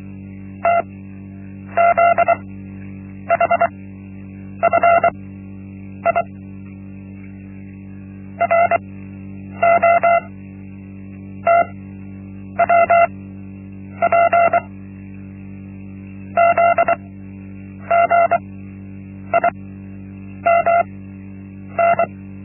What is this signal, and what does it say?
Morse code, heard as a keyed tone, about 24 wpm, spaced as for 8 wpm, 680 Hz. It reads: TZHFI ROTWP ZGIMN
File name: Linksmakalnis-1